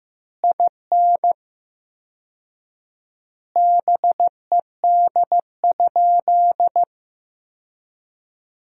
Morse code, heard as a keyed tone, about 15 wpm, 700 Hz. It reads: IN BED?
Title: 4x Word-Spacing - Sentences from Top 500 Words - 15wpm